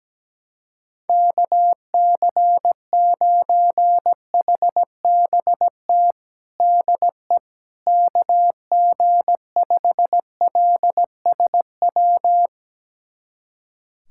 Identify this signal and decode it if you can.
Morse code, heard as a keyed tone, about 17 wpm, 700 Hz. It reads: KC9HBT DE KG5LSW
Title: USA Callsign DE Callsign 17wpm